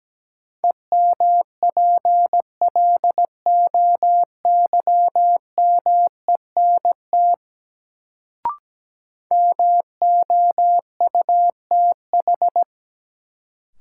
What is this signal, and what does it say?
Morse code, heard as a keyed tone, about 17 wpm, 700 Hz. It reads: EMPLOYMENT MOUTH
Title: Sentences from Top 1500 Words - New Word Drill Alt - 17wpm